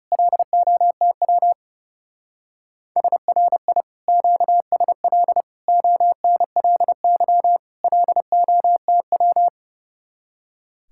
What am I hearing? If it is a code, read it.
Morse code, heard as a keyed tone, about 36 wpm, 700 Hz. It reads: LOTW SRI QSL ONLY LOTW
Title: Speed-Racing Common QSO Phrases Encore - 30wpm